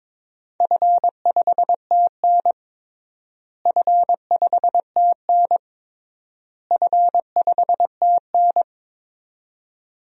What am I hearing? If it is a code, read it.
Morse code, heard as a keyed tone, about 22 wpm, 700 Hz. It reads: F5TN F5TN F5TN
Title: DXCC-100 Call Signs - 4 Characters - Repeated 3x - 22wpm